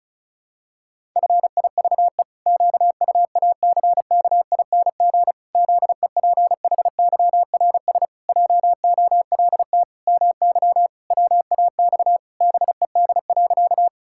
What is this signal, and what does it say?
Morse code, heard as a keyed tone, about 36 wpm, 700 Hz. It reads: FIVE QUACKING ZEPHYRS JOLT MY WAX BED.
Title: Pangram Challenge v2 - 35wpm